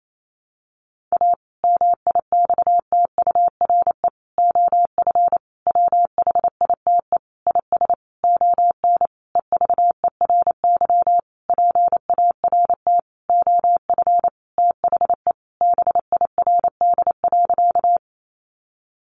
Morse code, heard as a keyed tone, about 28 wpm, 700 Hz. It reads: A MIXTURE OF WHITE IS ON EVERY PART OF THE BIRD.
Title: Sentences from Top 1500 Words 28wpm